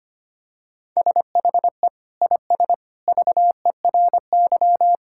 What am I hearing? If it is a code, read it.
Morse code, heard as a keyed tone, about 25 wpm, 700 Hz. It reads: SHE IS VERY
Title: Sentences from Top 1000 Words - Review Part 6 - 25wpm